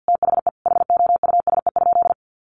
Morse code, heard as a keyed tone, about 50 wpm, 700 Hz. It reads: THE HOUSE?
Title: Sentences from Top 1300 Words 50wpm